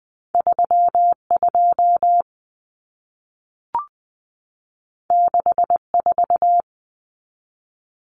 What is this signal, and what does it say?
Morse code, heard as a keyed tone, about 20 wpm, 700 Hz. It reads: 32 64
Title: Common Cut Numbers - 2-Digit 20wpm